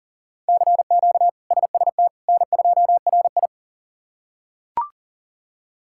Morse code, heard as a keyed tone, about 37 wpm, 700 Hz. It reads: CQ SST N2RI
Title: K1USN SST QSO Practice - 40wpm